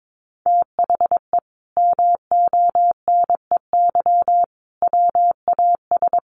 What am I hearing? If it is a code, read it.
Morse code, heard as a keyed tone, about 22 wpm, 700 Hz. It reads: THE MONEY WAS
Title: Sentences from Top 1400 Words 22wpm